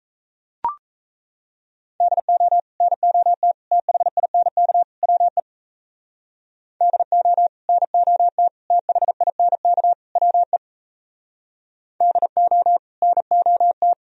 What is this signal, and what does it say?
Morse code, heard as a keyed tone, about 34 wpm, 700 Hz. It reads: DO NOT THINK WE DO NOT THINK WE DO NOT
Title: Speed-Racing - Sets of 4 Words 28wpm